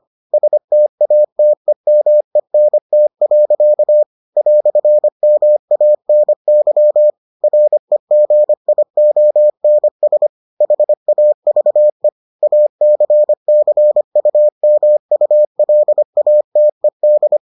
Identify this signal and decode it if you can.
Morse code, heard as a keyed tone, about 25 wpm, 600 Hz. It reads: STATEMENT. "MANY REGIONS HAVE ACCUMULATED